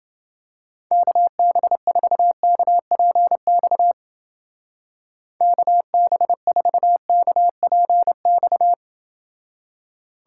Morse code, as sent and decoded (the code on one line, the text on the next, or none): -.- -... ....- -.- .--. -..- / -.- -... ....- -.- .--. -..-
KB4KPX KB4KPX